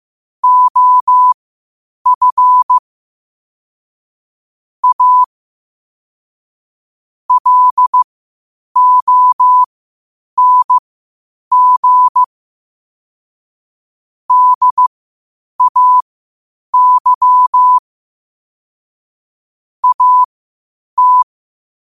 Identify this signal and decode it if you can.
Morse code, heard as a keyed tone, about 15 wpm, spaced as for 8 wpm, 1000 Hz. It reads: OF A LONG DAY AT
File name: qotdfiles / 8wpm